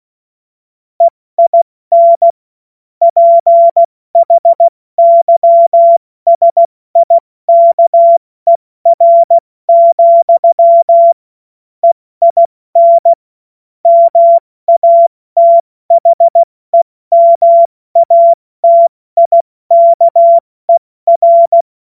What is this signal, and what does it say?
Morse code, heard as a keyed tone, about 16 wpm, spaced as for 14 wpm, 680 Hz. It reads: EIN PHYSIKER, EIN MATHEMATIKER
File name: Morsekurs / Witze / Witz 12. April 2022 70 F80